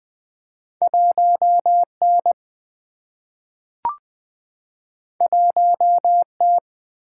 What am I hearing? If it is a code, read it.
Morse code, heard as a keyed tone, about 20 wpm, 700 Hz. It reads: 1 N 1 T
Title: Single-Letter Single-Number 20wpm